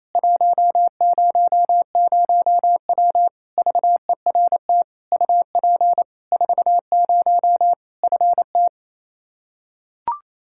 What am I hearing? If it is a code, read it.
Morse code, heard as a keyed tone, about 28 wpm, 700 Hz. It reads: 100W VERT UP 40 FT
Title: K1USN SST QSO Practice 28wpm